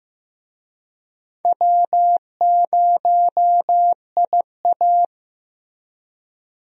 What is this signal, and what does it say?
Morse code, heard as a keyed tone, about 16 wpm, 700 Hz. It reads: W0IA
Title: USA Call Signs - 4 Characters - Repeated 3x - 15wpm